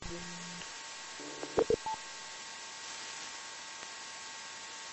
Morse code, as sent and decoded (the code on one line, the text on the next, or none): none